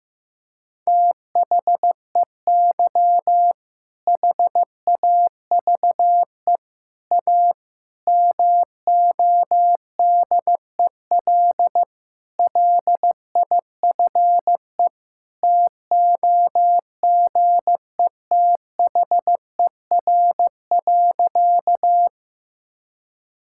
Morse code, heard as a keyed tone, about 15 wpm, 700 Hz. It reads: THEY HAVE A MODEL LIFE TOGETHER.
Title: Sentences from Top 1000 Words - Review Part 5 - 15wpm